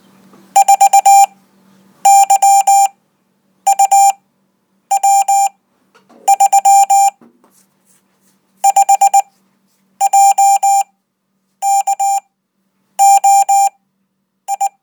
Morse code, heard as a keyed tone, about 20 wpm, spaced as for 9 wpm, 790 Hz. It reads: 4YUW3 5JKOI